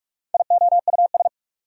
Morse code, heard as a keyed tone, about 58 wpm, 700 Hz. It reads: IOUS